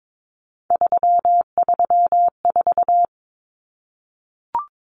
Morse code, heard as a keyed tone, about 22 wpm, 700 Hz. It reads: 334
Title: Common Cut Numbers - 3-Digit 22wpm